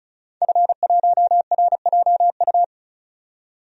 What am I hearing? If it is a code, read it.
Morse code, heard as a keyed tone, about 34 wpm, 700 Hz. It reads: F1RJU
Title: DXCC-100 Call Signs - 5 Characters - 35wpm